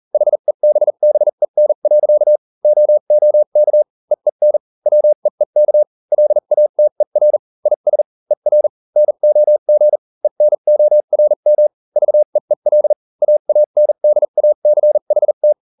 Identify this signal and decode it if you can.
Morse code, heard as a keyed tone, about 40 wpm, spaced as for 35 wpm, 600 Hz. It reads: HEBBEN. OOK EEN WEEK LATER IS ER NOG ENORM VEEL AANDACHT